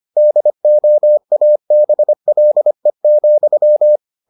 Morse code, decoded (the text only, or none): DOABLE,